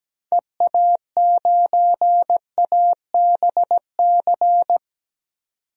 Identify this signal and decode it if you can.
Morse code, heard as a keyed tone, about 17 wpm, 700 Hz. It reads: EA9ABC
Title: DXCC-100 Call Signs - 6 Characters - 17wpm